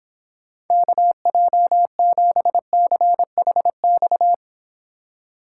Morse code, heard as a keyed tone, about 25 wpm, 700 Hz. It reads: KJ7CHX